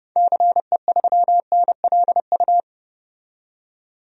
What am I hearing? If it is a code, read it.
Morse code, heard as a keyed tone, about 32 wpm, 700 Hz. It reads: CE3NLU
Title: DXCC-100 Call Signs - 6 Characters - 30wpm